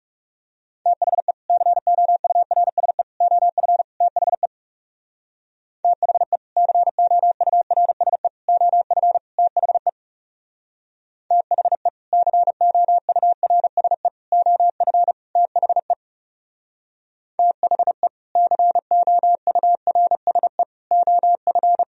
Morse code, heard as a keyed tone, about 36 wpm, 700 Hz. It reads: THE COURSE OF THE THE COURSE OF THE THE COURSE OF THE THE COURSE OF
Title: Speed-Racing - Sets of 4 Words 30wpm